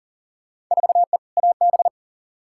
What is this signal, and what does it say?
Morse code, heard as a keyed tone, about 40 wpm, 700 Hz. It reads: 4E AB
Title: Field Day Exchanges Encore - 40wpm